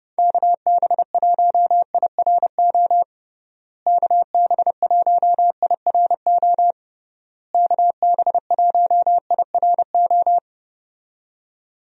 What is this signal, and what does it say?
Morse code, heard as a keyed tone, about 29 wpm, 700 Hz. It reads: KB1IRO KB1IRO KB1IRO
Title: USA Call Signs - 6 Characters - Repeated 3x - 30wpm